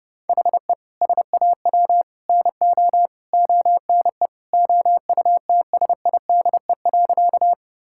Morse code, heard as a keyed tone, about 30 wpm, 700 Hz. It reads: HE SAW NO ONE OUTSIDE.
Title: Sentences from Top 1000 Words - Review Part 6 - 30wpm